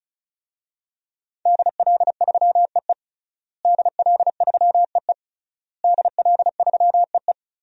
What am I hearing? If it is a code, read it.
Morse code, heard as a keyed tone, about 35 wpm, 700 Hz. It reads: DL3EE DL3EE DL3EE
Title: DXCC-100 Call Signs - 5 Characters - Repeated 3x - 35wpm